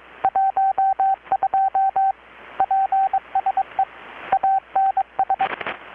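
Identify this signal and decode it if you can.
Morse code, heard as a keyed tone, about 21 wpm, 770 Hz, with radio noise. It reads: 12 PSE ANS
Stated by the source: Morse from KPH's Night of Nights XX maritime radiotelegraphy event, received in Chiapas, Mexico about 3,800 km away on a Kenwood R-600 with a grounded 10 m long-wire antenna.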